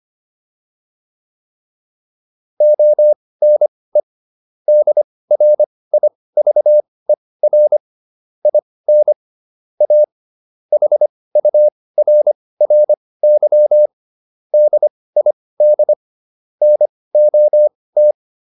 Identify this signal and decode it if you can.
Morse code, heard as a keyed tone, about 25 wpm, spaced as for 18 wpm, 600 Hz. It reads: ONE DRIVER IN A HURRY DID NOT